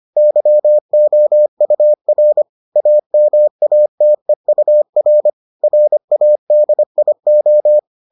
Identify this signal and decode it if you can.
Morse code, heard as a keyed tone, about 25 wpm, 600 Hz. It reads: YOUR AMATEUR RADIO